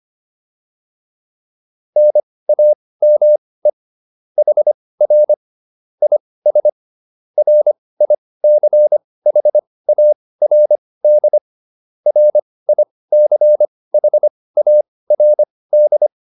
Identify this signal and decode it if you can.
Morse code, heard as a keyed tone, about 25 wpm, spaced as for 18 wpm, 600 Hz. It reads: NAME HR IS RICHARD RICHARD